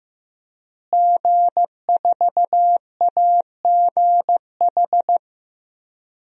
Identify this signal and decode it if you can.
Morse code, heard as a keyed tone, about 16 wpm, 700 Hz. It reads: G4AGH
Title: DXCC-100 Call Signs - 5 Characters - Repeated 3x - 15wpm